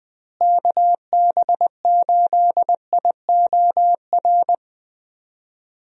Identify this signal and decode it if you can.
Morse code, heard as a keyed tone, about 20 wpm, 700 Hz. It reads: KB8IOR
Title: USA Call Signs - 6 Characters - Repeated 3x - 20wpm